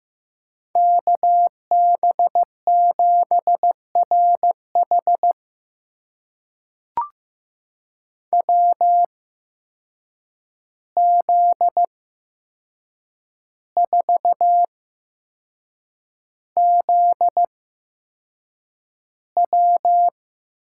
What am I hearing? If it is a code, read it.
Morse code, heard as a keyed tone, about 15 wpm, spaced as for 6 wpm, 700 Hz. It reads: KB7RH WZ4ZW
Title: USA Call Signs - 5 Characters - 8x Character Spacing - 15wpm